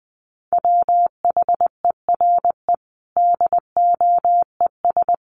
Morse code, heard as a keyed tone, about 20 wpm, 700 Hz. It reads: WHERE DOES
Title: Sentences from Top 1200 Words 20wpm